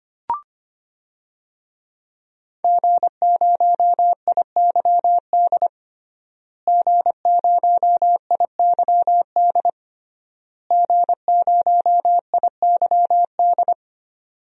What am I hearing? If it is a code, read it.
Morse code, heard as a keyed tone, about 25 wpm, 700 Hz. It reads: G0IYD G0IYD G0IYD
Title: DXCC-100 Call Signs - 5 Characters - Repeated 3x - 25wpm